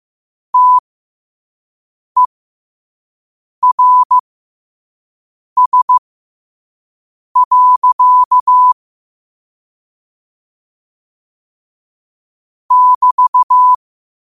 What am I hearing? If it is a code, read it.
Morse code, heard as a keyed tone, about 15 wpm, spaced as for 5 wpm, 1000 Hz. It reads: TERS. <BT>
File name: qotdfiles / 5wpm